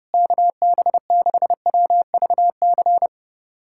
Morse code, heard as a keyed tone, about 30 wpm, 700 Hz. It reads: KB6WVC
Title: USA Callsign DE Callsign - 6 Characters - 30wpm